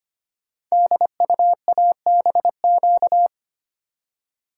Morse code, heard as a keyed tone, about 25 wpm, 700 Hz. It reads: DUABQ
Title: Code Groups - 5 Letters - 25wpm